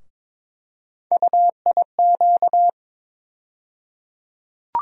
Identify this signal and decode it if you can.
Morse code, heard as a keyed tone, about 22 wpm, 700 Hz. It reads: UIQ
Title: USA Call Sign Postfixes - 3 Letters 22wpm